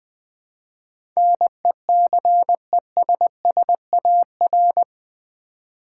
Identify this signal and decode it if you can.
Morse code, heard as a keyed tone, about 20 wpm, 700 Hz. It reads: NECESSAR
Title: Build A Word - 4 Letters At A Time - 11-Letter Words - 20wpm